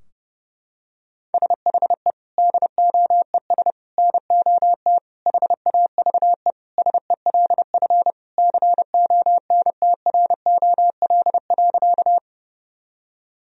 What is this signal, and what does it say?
Morse code, heard as a keyed tone, about 30 wpm, 700 Hz. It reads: SHE DOES NOT HAVE SELF CONTROL.
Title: Sentences from Top 1000 Words - Review Part 3 - 30wpm